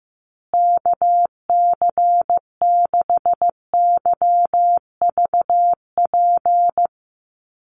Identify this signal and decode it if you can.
Morse code, heard as a keyed tone, about 15 wpm, 700 Hz. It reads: KC6YVP
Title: USA Call Signs Repeated 3x 15wpm